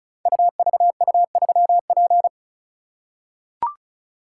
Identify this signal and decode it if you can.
Morse code, heard as a keyed tone, about 36 wpm, 700 Hz. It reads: UVU3P
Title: Code Groups - 5 Letter:Numbers - 35wpm